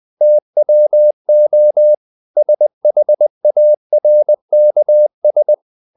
Morse code, heard as a keyed tone, about 20 wpm, 600 Hz. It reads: TWO SHARKS
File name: the old man and the sea part 2 20wpm